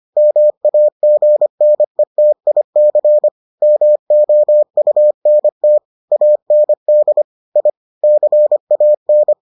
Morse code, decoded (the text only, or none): MAGNETIC MOUNT AND I CAN